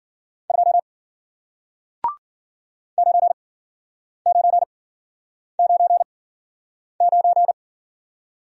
Morse code, heard as a keyed tone, about 53 wpm, spaced as for 9 wpm, 700 Hz. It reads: 2 9999